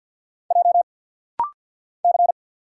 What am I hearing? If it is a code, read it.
Morse code, heard as a keyed tone, about 50 wpm, 700 Hz. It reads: J C
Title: Single Letters - Rapid-Fire 50wpm